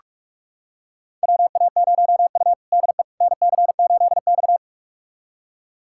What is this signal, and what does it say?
Morse code, heard as a keyed tone, about 46 wpm, 700 Hz. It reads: WA0U DE NC8X
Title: USA Callsign DE Callsign - 4 Characters - 45wpm